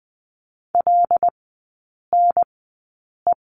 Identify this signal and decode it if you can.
Morse code, heard as a keyed tone, about 20 wpm, spaced as for 8 wpm, 700 Hz. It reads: LNE
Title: 074 Lesson 13e - Letter L - Three Characters 20wpm